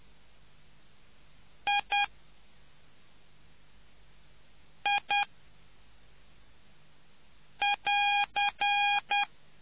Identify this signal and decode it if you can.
Morse code, heard as a keyed tone, about 10 wpm, 810 Hz. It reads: I I <AR>